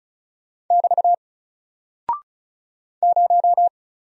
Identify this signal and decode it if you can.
Morse code, heard as a keyed tone, about 36 wpm, 700 Hz. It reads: <BT> 0
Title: Single Character (Letter:Number:Punctuation:Prosign) 35wpm